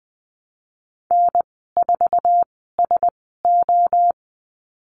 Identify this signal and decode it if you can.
Morse code, heard as a keyed tone, about 19 wpm, spaced as for 14 wpm, 700 Hz. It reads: N4SO
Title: 055 Lesson 10g - Letter r - Call Signs 20wpm